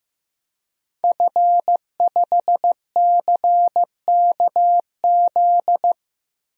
Morse code, heard as a keyed tone, about 15 wpm, 700 Hz. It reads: F5CKZ
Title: DXCC-100 Call Signs - 5 Characters - Repeated 2x - 15wpm